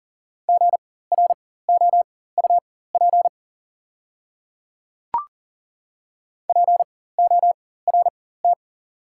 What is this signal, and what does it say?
Morse code, heard as a keyed tone, about 40 wpm, spaced as for 19 wpm, 700 Hz. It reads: GROUP PORT
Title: 4x Character Spacing - Top 300 Words - 40wpm